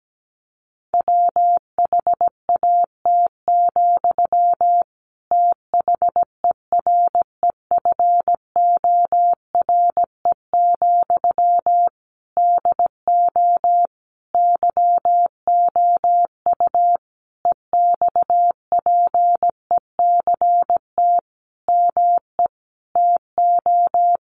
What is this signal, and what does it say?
Morse code, heard as a keyed tone, about 17 wpm, 700 Hz. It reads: WHAT, THEREFORE, DO YOU EXPECT ME TO
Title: Sentences from Top 1100 Words 17wpm